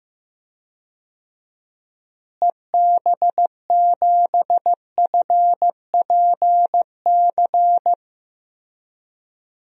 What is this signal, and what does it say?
Morse code, heard as a keyed tone, about 15 wpm, 700 Hz. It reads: EB7FPC
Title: DXCC-100 Call Signs - 6 Characters - Repeated 3x - 15wpm